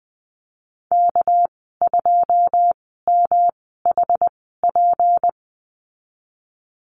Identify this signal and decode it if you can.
Morse code, heard as a keyed tone, about 20 wpm, spaced as for 15 wpm, 700 Hz. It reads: K2MHP
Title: 207 Lesson 33g - Letter j - Call Signs 20wpm